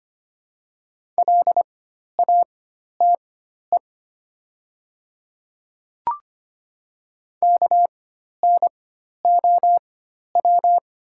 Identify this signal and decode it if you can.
Morse code, heard as a keyed tone, about 25 wpm, spaced as for 12 wpm, 700 Hz. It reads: LATE KNOW
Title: 4x Character Spacing - Top 300 Words - 25wpm